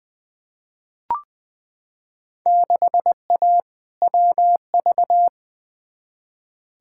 Morse code, heard as a keyed tone, about 20 wpm, 700 Hz. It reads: 6A WV
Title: Field Day Exchanges - 20wpm